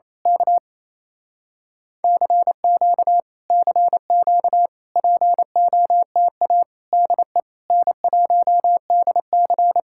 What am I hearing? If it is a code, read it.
Morse code, heard as a keyed tone, about 27 wpm, 700 Hz. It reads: K CQ CQ POTA DE N1DC
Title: Speed-Racing Common QSO Phrases - 28wpm